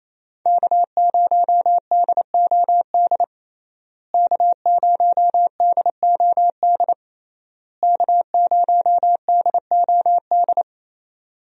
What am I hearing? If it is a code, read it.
Morse code, heard as a keyed tone, about 27 wpm, 700 Hz. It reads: K0DOD K0DOD K0DOD
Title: USA Call Signs - 5 Characters - Repeated 3x - 28wpm